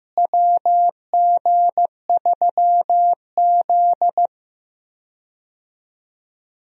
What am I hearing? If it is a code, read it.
Morse code, heard as a keyed tone, about 16 wpm, 700 Hz. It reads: WG3Z
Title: USA Call Signs - 4 Characters - Repeated 2x - 15wpm